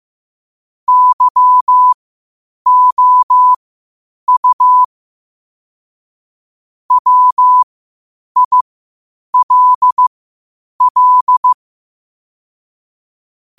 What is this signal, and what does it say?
Morse code, heard as a keyed tone, about 15 wpm, spaced as for 8 wpm, 1000 Hz. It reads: YOU WILL
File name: qotdfiles / 8wpm